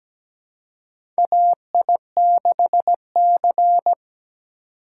Morse code, heard as a keyed tone, about 17 wpm, 700 Hz. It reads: AI6C
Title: USA Call Signs - 4 Characters - 4x Character Spacing - 17wpm